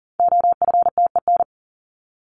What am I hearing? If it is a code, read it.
Morse code, heard as a keyed tone, about 40 wpm, 700 Hz. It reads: OFTEN
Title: Top 300 Words Repeated 40wpm